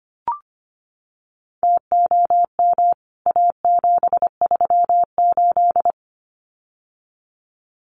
Morse code, heard as a keyed tone, about 25 wpm, 700 Hz. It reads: TOM A738
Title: CWT Exchange 25wpm